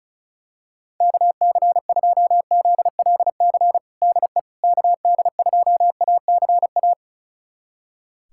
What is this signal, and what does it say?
Morse code, heard as a keyed tone, about 36 wpm, 700 Hz. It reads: KC2ZLC DE KD2ACA
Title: USA Callsign DE Callsign 35wpm